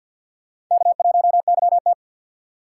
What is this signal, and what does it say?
Morse code, heard as a keyed tone, about 52 wpm, 700 Hz. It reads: K1YT
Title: K1USN SST QSO Practice - 50wpm